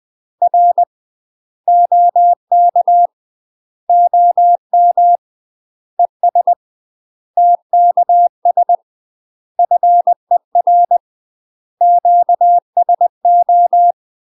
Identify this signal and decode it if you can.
Morse code, heard as a keyed tone, about 20 wpm, 700 Hz. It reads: R OK OM ES TKS FER QSO
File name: UK MASTER QSOs TRAINING / 20WPM / UK MASTER QSOs 0010